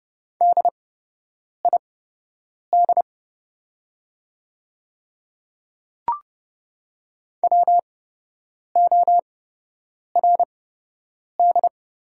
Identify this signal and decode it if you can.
Morse code, heard as a keyed tone, about 30 wpm, spaced as for 8 wpm, 700 Hz. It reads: DID WORD
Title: 8x Character Spacing - Top 200 Words - 30wpm